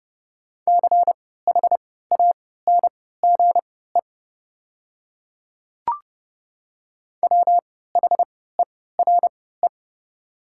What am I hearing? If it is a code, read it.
Morse code, heard as a keyed tone, about 31 wpm, spaced as for 17 wpm, 700 Hz. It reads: CHANGE WHERE
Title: 3x Character Spacing - Top 300 Words - 30wpm